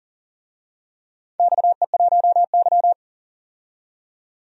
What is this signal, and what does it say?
Morse code, heard as a keyed tone, about 40 wpm, 700 Hz. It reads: XE1Y
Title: DXCC-100 Call Signs - 4 Characters - 40wpm